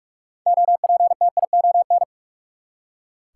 Morse code, heard as a keyed tone, about 44 wpm, 700 Hz. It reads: OPTION
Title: Six Letter Words - 45wpm